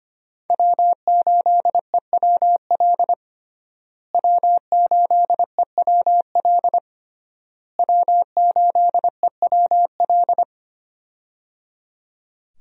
Morse code, heard as a keyed tone, about 25 wpm, 700 Hz. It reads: W8EWL W8EWL W8EWL
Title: USA Call Signs - 5 Characters - Repeated 3x - 25wpm